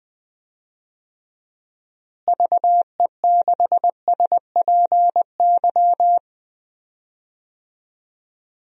Morse code, heard as a keyed tone, about 20 wpm, 700 Hz. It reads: VE6SPY
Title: DXCC-100 Call Signs - 6 Characters - Repeated 2x - 20wpm